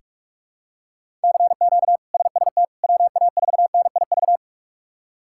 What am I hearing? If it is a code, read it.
Morse code, heard as a keyed tone, about 46 wpm, 700 Hz. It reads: CQ SST WA4NIV
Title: K1USN SST QSO Practice - 45wpm